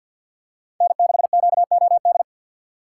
Morse code, heard as a keyed tone, about 52 wpm, 690 Hz. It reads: N6QOD
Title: USA Call Signs - 6x Character Spacing - 50wpm